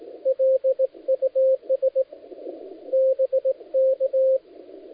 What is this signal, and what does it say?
Morse code, heard as a keyed tone, about 18 wpm, 500 Hz, with radio noise. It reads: LUS BK
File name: dlazaawansowanych / nagraniazpasma / titawaNETcontrol